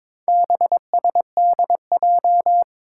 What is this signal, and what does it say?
Morse code, heard as a keyed tone, about 22 wpm, 700 Hz. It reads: BSDJ
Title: Four Challenging Letter-Number Combinations 22wpm